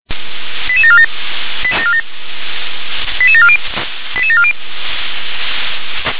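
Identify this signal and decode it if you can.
Five-tone selective call (ZVEI), heard as quick runs of five tones, tones of about 70 ms. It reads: 88747 88740 88740
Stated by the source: ZVEI five-tone selective call 88730, received by SDR.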